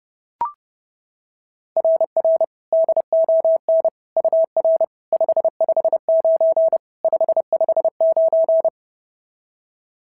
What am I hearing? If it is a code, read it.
Morse code, heard as a keyed tone, about 26 wpm, 650 Hz. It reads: RR DON UR 559 559
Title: SOTA QSO Practice 30wpm